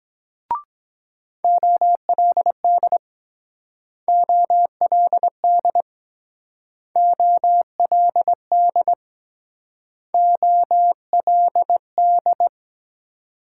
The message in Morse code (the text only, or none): OLD OLD OLD OLD